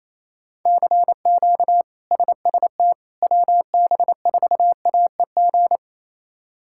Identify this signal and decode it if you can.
Morse code, heard as a keyed tone, about 28 wpm, 700 Hz. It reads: CQ SST WB4AEG
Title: K1USN SST QSO Practice - 28wpm